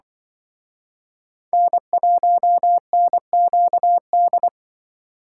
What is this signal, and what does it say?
Morse code, heard as a keyed tone, about 24 wpm, 700 Hz. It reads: N1NQD
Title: Speed Racing - USA 1x3 Call Signs 20wpm